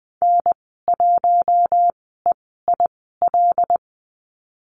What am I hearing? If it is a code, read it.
Morse code, heard as a keyed tone, about 20 wpm, spaced as for 15 wpm, 700 Hz. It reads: N1EIL
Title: 082 Lesson 14g - Number 2 - Call Signs 20wpm